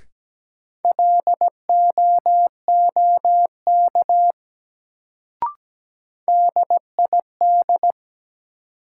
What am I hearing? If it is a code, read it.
Morse code, heard as a keyed tone, about 17 wpm, 700 Hz. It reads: LOOK DID
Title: Sets of 2 Words - New Word Drill - 17wpm